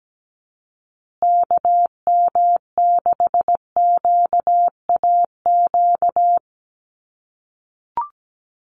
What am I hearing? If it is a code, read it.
Morse code, heard as a keyed tone, about 17 wpm, 700 Hz. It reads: KM6QAQ
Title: USA Call Signs Repeated 3x 17wpm